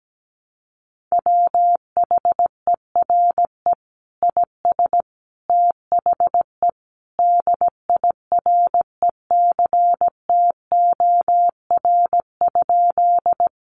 Morse code, heard as a keyed tone, about 17 wpm, 700 Hz. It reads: WHERE IS THE DIRECTOR?
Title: Sentences from Top 1300 Words 17wpm